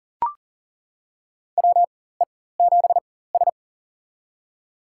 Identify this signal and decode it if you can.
Morse code, heard as a keyed tone, about 40 wpm, spaced as for 19 wpm, 700 Hz. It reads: WE7S
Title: USA Call Signs - 4 Characters - 4x Character Spacing - 40wpm